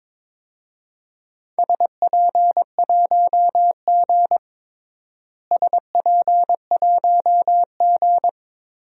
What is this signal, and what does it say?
Morse code, heard as a keyed tone, about 22 wpm, 700 Hz. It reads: SP1G SP1G
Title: DXCC-100 Call Signs - 4 Characters - Repeated 2x - 22wpm